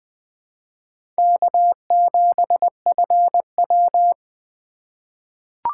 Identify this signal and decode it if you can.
Morse code, heard as a keyed tone, about 21 wpm, 700 Hz. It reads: K7FW
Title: USA Call Signs - 4 Characters - 4x Character Spacing - 20wpm